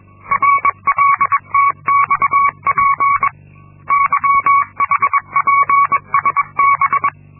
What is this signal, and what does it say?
Morse code, heard as a keyed tone, about 22 wpm, 1120 Hz. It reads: RLTXP YHPSB
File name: Linksmakalnis-2